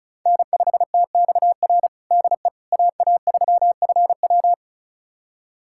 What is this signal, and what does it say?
Morse code, heard as a keyed tone, about 36 wpm, 690 Hz. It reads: N5TXR DE AA3FW
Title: USA Callsign DE Callsign - 5 Characters - 35wpm